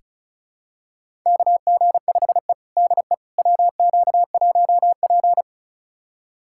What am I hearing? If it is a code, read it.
Morse code, heard as a keyed tone, about 34 wpm, 700 Hz. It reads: KG5E DE WQ1P
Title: USA Callsign DE Callsign - 4 Characters - 35wpm